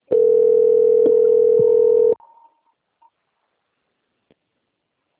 Ringback tone (North America), 440 + 480 Hz.
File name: out- - - - - .10835